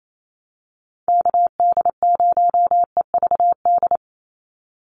Morse code, heard as a keyed tone, about 28 wpm, 700 Hz. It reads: KD0EVD
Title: USA Call Signs 28wpm